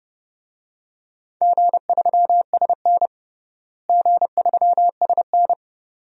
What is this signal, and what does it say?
Morse code, heard as a keyed tone, about 32 wpm, 700 Hz. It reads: G3SN G3SN
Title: DXCC-100 Call Signs - 4 Characters - Repeated 2x - 30wpm